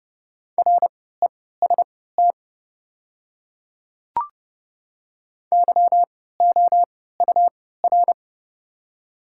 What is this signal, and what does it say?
Morse code, heard as a keyed tone, about 30 wpm, spaced as for 17 wpm, 690 Hz. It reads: REST YOUR
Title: 3x Character Spacing - Top 500 Words - 30wpm